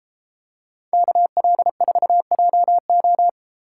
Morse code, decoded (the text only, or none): KL4JO